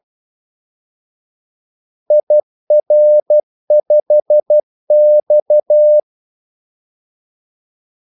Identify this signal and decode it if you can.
Morse code, heard as a keyed tone, about 12 wpm, 600 Hz. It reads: IR5X
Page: 2 ekstra ordmellemrum mellem kaldesignalerne